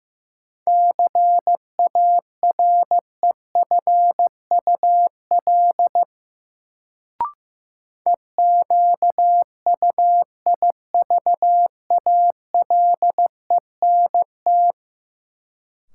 Morse code, heard as a keyed tone, about 15 wpm, 700 Hz. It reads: CAREFUL EQUIVALENT
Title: Sentences from Top 1600 Words - New Word Drill - 15wpm